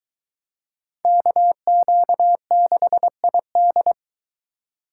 Morse code, heard as a keyed tone, about 23 wpm, 700 Hz. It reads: KQ6ID